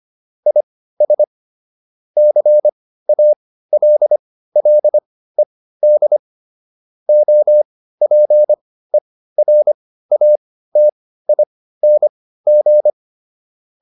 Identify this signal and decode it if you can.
Morse code, heard as a keyed tone, about 25 wpm, spaced as for 15 wpm, 600 Hz. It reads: IS CALLED OPERATING